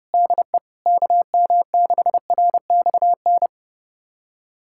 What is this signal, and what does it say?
Morse code, heard as a keyed tone, about 32 wpm, 700 Hz. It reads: DE KM6RXN
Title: USA Callsign DE Callsign - 6 Characters - 30wpm